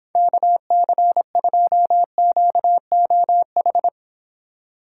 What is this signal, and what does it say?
Morse code, heard as a keyed tone, about 25 wpm, 700 Hz. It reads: KC2QOH